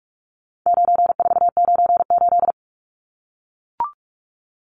Morse code, heard as a keyed tone, about 44 wpm, 700 Hz. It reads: 9498
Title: Numbers - 4-Digit 45wpm